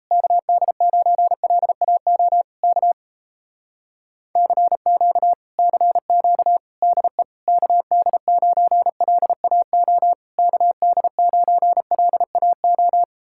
Morse code, heard as a keyed tone, about 38 wpm, 700 Hz. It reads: KD9LAO K CQ CQ DE KD9LAO KD9LAO